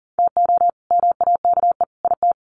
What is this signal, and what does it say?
Morse code, heard as a keyed tone, about 40 wpm, 700 Hz. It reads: TO MAKE IT
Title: Sets of 3 Words 40wpm